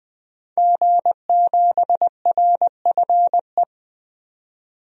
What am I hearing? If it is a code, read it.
Morse code, heard as a keyed tone, about 20 wpm, 700 Hz. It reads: G7RFE
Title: DXCC-100 Call Signs - 5 Characters - 20wpm